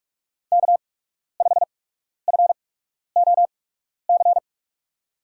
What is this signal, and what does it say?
Morse code, heard as a keyed tone, about 44 wpm, spaced as for 12 wpm, 700 Hz. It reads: K5FOC
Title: USA Call Signs - 5 Characters - 8x Character Spacing - 45wpm